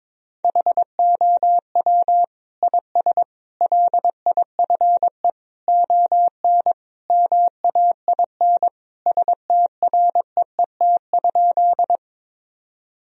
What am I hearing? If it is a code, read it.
Morse code, heard as a keyed tone, about 22 wpm, 700 Hz. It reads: HOW IS LIFE ON MAIN STREET?
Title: Sentences from Top 1000 Words - Review Part 2 - 22wpm